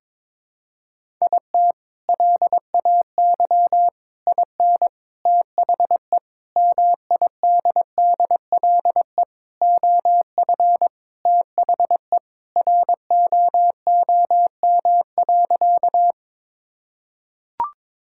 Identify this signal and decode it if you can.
Morse code, heard as a keyed tone, about 22 wpm, 700 Hz. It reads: IT LAY IN THE MIDDLE OF THE ROOM.
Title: Sentences from Top 1000 Words - Review Part 6 - 22wpm